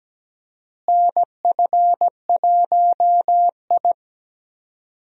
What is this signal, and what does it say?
Morse code, heard as a keyed tone, about 17 wpm, 700 Hz. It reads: NF1I
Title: USA Call Signs - 4 Characters - Repeated 2x - 17wpm